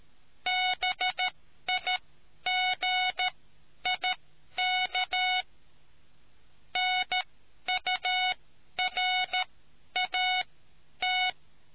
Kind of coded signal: Morse code